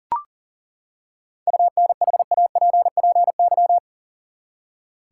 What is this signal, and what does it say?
Morse code, heard as a keyed tone, about 40 wpm, 700 Hz. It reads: UNHAPPY
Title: Seven Letter Words - 40wpm